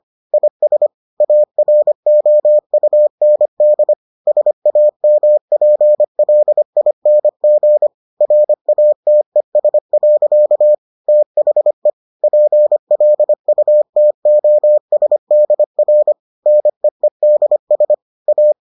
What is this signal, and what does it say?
Morse code, heard as a keyed tone, about 25 wpm, 600 Hz. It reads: IS AROUND SAMPLING RATES. THE PLUTOSDR NEEDS A